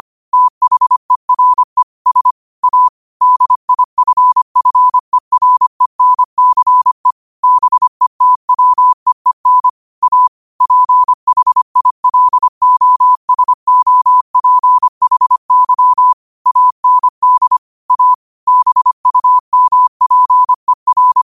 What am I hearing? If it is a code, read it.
Morse code, heard as a keyed tone, about 25 wpm, 1000 Hz. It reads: THERE S A DIFFERENCE BETWEEN A PHILOSOPHY AND A BUMPER
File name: qotdfiles / 25wpm